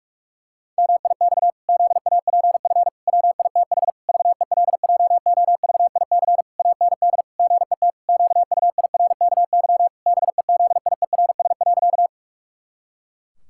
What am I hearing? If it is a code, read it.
Morse code, heard as a keyed tone, about 48 wpm, 700 Hz. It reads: MIX ZAPF WITH VELJOVIC AND GET QUIRKY BEZIERS.
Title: Pangram Challenge v2 - 45wpm